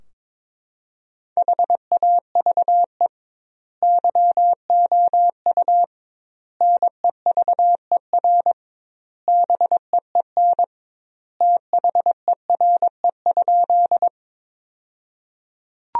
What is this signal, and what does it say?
Morse code, heard as a keyed tone, about 21 wpm, 700 Hz. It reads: HAVE YOU NEVER BEEN THERE?
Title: 2x Word-Spacing - Sentences from Top 300 Words - 22wpm